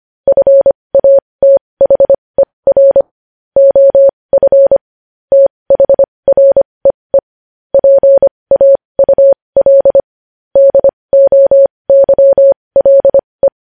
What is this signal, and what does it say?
Morse code, heard as a keyed tone, about 25 wpm, spaced as for 20 wpm, 550 Hz. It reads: FATHER OF THREE PAUL DOYLE
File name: podcast / News headlines-1x-25x20